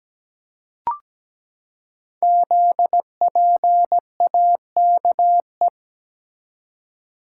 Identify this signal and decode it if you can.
Morse code, heard as a keyed tone, about 17 wpm, 690 Hz. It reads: ZPAKE
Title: Code Groups - 5 Letter:Numbers - 17wpm